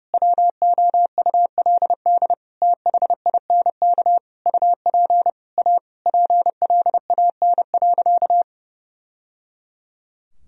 Morse code, heard as a keyed tone, about 30 wpm, 700 Hz. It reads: WOULD THINK UP A PLAN.
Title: Sentences from Top 1000 Words - Review Part 2 - 30wpm